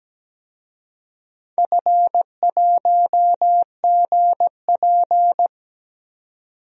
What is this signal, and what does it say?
Morse code, heard as a keyed tone, about 17 wpm, 700 Hz. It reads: F1GP
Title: DXCC-100 Call Signs - 4 Characters - 17wpm